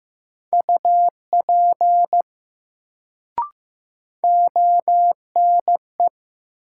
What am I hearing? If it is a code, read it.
Morse code, heard as a keyed tone, about 15 wpm, 700 Hz. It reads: UP ONE
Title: Sentences from Top 100 Words - New Word Drill - 15wpm